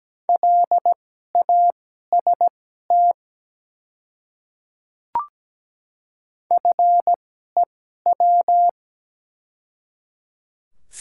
Morse code, heard as a keyed tone, about 17 wpm, spaced as for 12 wpm, 690 Hz. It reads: LAST FEW
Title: 2x Character Spacing - Top 300 Words - 17wpm